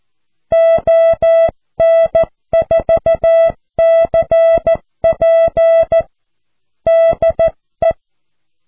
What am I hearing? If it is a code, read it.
Morse code, heard as a keyed tone, about 14 wpm, 660 Hz. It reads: ON4CP DE
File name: cw / QSO